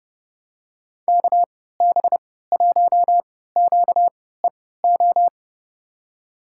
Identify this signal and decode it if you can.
Morse code, heard as a keyed tone, about 31 wpm, spaced as for 17 wpm, 700 Hz. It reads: KB1QEO
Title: USA Call Signs - 3x Character Spacing - 30wpm